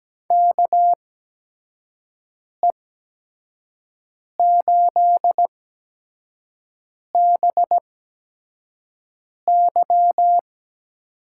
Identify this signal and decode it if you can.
Morse code, heard as a keyed tone, about 17 wpm, spaced as for 5 wpm, 700 Hz. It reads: KE8BY
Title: USA Call Signs - 6 Characters - 8x Character Spacing - 17wpm